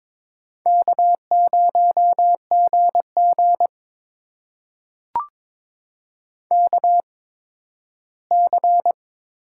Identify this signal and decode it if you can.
Morse code, heard as a keyed tone, about 22 wpm, spaced as for 13 wpm, 700 Hz. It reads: K0GG KC
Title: USA Call Signs - 8x Character Spacing - 22wpm